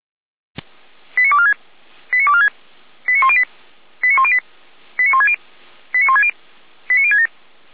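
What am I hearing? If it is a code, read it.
Five-tone selective call (ZVEI), heard as quick runs of five tones, tones of about 80 ms. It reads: 89246 89246 89108 89108 89160 89160 89067
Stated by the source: ZVEI five-tone call received by SDR.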